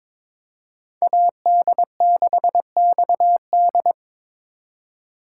Morse code, heard as a keyed tone, about 26 wpm, 700 Hz. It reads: AD6XD
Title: Speed Racing - USA 2x2 Call Signs 22wpm